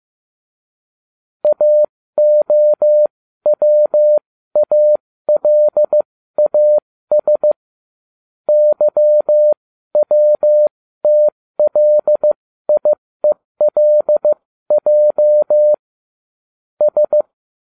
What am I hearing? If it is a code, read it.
Morse code, heard as a keyed tone, about 15 wpm, spaced as for 13 wpm, 600 Hz. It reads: AOWALAS YWTLIELJ S